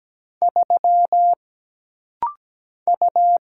Morse code, heard as a keyed tone, about 17 wpm, 700 Hz. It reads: 3 U
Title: Single Letter-Number Alt - Rapid-Fire 17wpm